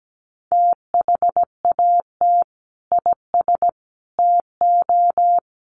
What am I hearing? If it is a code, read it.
Morse code, heard as a keyed tone, about 17 wpm, 700 Hz. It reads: THAT IS TO
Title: Sentences from Top 700 Words 17wpm